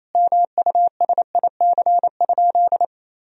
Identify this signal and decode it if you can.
Morse code, heard as a keyed tone, about 27 wpm, 700 Hz. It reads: MUSIC?
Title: Sentences from Top 1000 Words - Review Part 3 - 28wpm